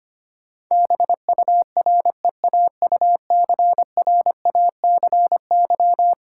Morse code, heard as a keyed tone, about 25 wpm, 700 Hz. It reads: BUREAUCRACY
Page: morse-code-ninja-practice-25wpm